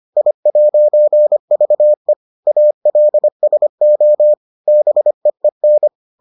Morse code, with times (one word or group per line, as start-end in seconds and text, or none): 0.2-2.1 I'VE
2.5-4.3 ALSO
4.7-5.9 BEEN